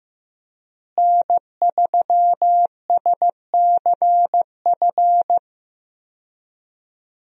Morse code, heard as a keyed tone, about 15 wpm, 700 Hz. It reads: N3SCF
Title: USA Call Signs - 5 Characters - Repeated 2x - 15wpm